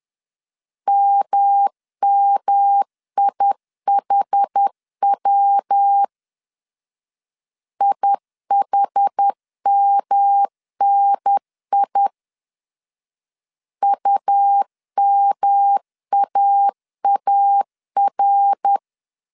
Morse code, heard as a keyed tone, about 10 wpm, 800 Hz. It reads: MMIHW IHMNI UMAAR
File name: CW / DET / 1 3 1500C S C 10wpm 3335